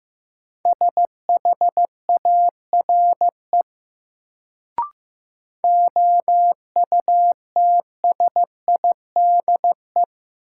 Morse code, heard as a keyed tone, about 15 wpm, 700 Hz. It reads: SHARE OUTSIDE
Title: Sentences from Top 900 Words - New Word Drill - 15wpm